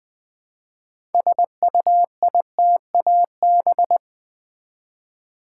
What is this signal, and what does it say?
Morse code, heard as a keyed tone, about 20 wpm, 700 Hz. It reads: SUITAB
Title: Build A Word - 2 Letters At A Time - 8-Letter Words - 20wpm